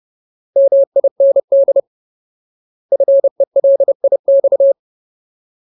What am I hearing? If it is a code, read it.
Morse code, heard as a keyed tone, about 30 wpm, 550 Hz. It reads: MIND FELIX
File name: Ultimate Word List / 30WPM Ultimate